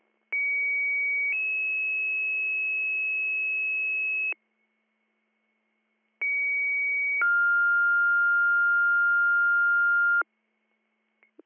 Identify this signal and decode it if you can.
Two-tone page, heard as a one-second tone, then a three-second tone, about 2250 then 2600 Hz, about 2250 then 1400 Hz. It is heard from 0.3 to 10.2 s.